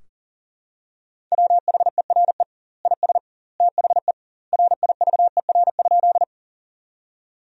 Morse code, heard as a keyed tone, about 37 wpm, 700 Hz. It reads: WHERE IS THE RIVER?
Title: 2x Word-Spacing - Sentences from Top 300 Words - 40wpm